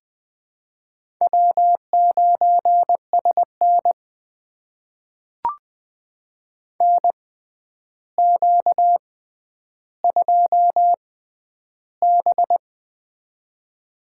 Morse code, heard as a keyed tone, about 21 wpm, spaced as for 10 wpm, 700 Hz. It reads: W9SN NQ2B
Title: USA Call Signs - 4 Characters - 6x Character Spacing - 20wpm